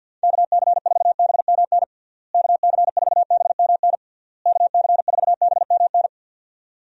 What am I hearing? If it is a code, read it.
Morse code, heard as a keyed tone, about 50 wpm, 700 Hz. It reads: KC4BMN KC4BMN KC4BMN
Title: USA Call Signs - 6 Characters - Repeated 3x - 50wpm